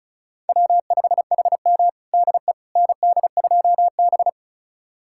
Morse code, heard as a keyed tone, about 34 wpm, 700 Hz. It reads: W5HM DE ND2B
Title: USA Callsign DE Callsign - 4 Characters - 35wpm